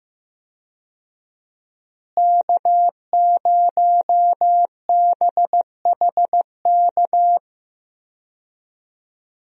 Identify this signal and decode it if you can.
Morse code, heard as a keyed tone, about 15 wpm, 700 Hz. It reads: K0BHK
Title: USA Call Signs - 5 Characters - Repeated 3x - 15wpm